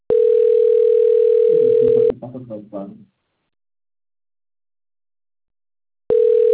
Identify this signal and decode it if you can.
Ringback tone (North America), 440 + 480 Hz, with speech.
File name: out- - - - - .48842